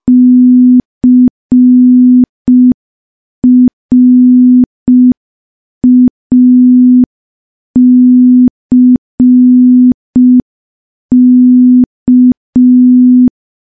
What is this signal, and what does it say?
Morse code, heard as a keyed tone, about 5 wpm, 250 Hz. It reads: CRACK
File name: morse / morse-training-05WPM